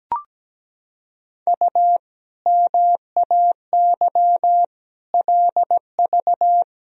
Morse code, heard as a keyed tone, about 17 wpm, 700 Hz. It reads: U MAY LV
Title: Sentences from Top 400 Words and Abbreviations 17wpm